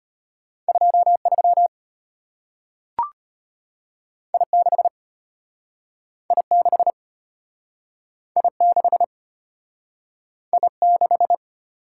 Morse code, heard as a keyed tone, about 33 wpm, 700 Hz. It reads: 23 I6 I6 I6 I6